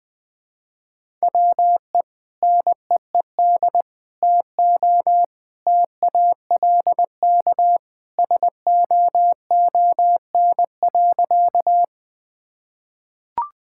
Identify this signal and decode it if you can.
Morse code, heard as a keyed tone, about 20 wpm, 700 Hz. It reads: WE NEED TO TALK SOON.
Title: Sentences with 2-Word Phrases - Spoken Beginning - 20wpm